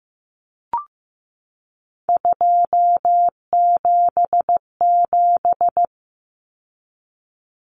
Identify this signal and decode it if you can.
Morse code, heard as a keyed tone, about 15 wpm, 700 Hz. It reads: 277
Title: Common Cut Numbers - 3-Digit 15wpm